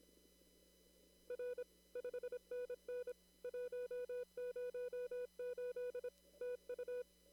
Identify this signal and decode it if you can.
Morse code, heard as a keyed tone, about 26 wpm, 490 Hz. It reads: R 5NN 108 TU
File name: CW / kt5j wpx cw-op-k5pi